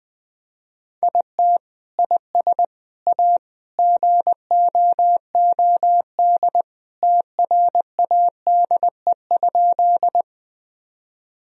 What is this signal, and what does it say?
Morse code, heard as a keyed tone, about 20 wpm, 700 Hz. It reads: IT IS A GOOD TRADE?
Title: Sentences from Top 1000 Words - Review Part 4 - 20wpm